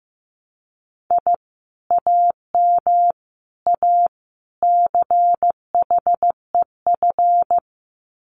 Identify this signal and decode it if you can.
Morse code, heard as a keyed tone, about 15 wpm, 700 Hz. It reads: I AM A CHEF
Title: Common QSO Phrases Encore 15wpm